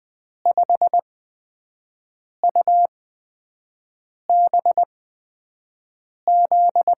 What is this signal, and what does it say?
Morse code, heard as a keyed tone, about 20 wpm, spaced as for 5 wpm, 700 Hz. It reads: 5UBZ
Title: USA Call Signs - 8x Character Spacing - 20wpm